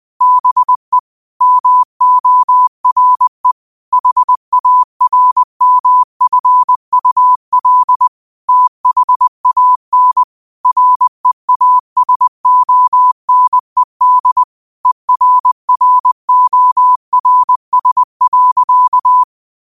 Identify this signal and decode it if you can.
Morse code, heard as a keyed tone, about 20 wpm, 1000 Hz. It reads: BE MORE HARMFUL THAN REASONED ERRORS.
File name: qotdfiles / 20wpm